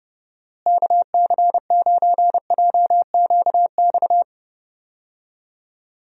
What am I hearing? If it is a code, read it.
Morse code, heard as a keyed tone, about 30 wpm, 700 Hz. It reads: KC9JQX
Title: USA Call Signs - 6 Characters - Repeated 3x - 30wpm